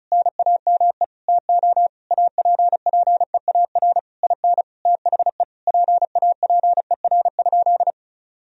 Morse code, heard as a keyed tone, about 34 wpm, 700 Hz. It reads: NAME TO APPEAR IN THE PAPER?
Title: Sentences from Top 1000 Words - Review Part 2 - 35wpm